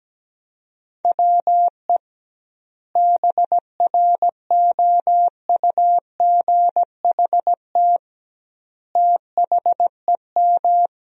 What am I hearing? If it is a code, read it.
Morse code, heard as a keyed tone, about 17 wpm, 700 Hz. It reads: WE BROUGHT THEM
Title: 2x Word-Spacing - Sentences from Top 500 Words - 17wpm